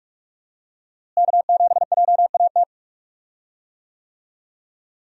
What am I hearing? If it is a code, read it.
Morse code, heard as a keyed tone, about 44 wpm, 700 Hz. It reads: K7JAT
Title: USA Call Signs - 5 Characters - Repeated 2x - 45wpm